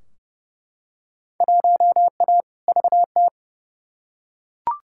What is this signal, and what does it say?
Morse code, heard as a keyed tone, about 30 wpm, 700 Hz. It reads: 1A VT
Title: Field Day Exchanges Encore - 30wpm